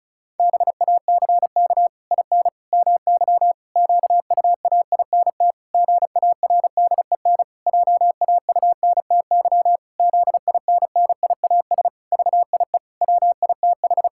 Morse code, heard as a keyed tone, about 36 wpm, 700 Hz. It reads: BACK IN MY QUAINT GARDEN JAUNTY ZINNIAS VIE WITH
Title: Pangram Challenge v2 - 35wpm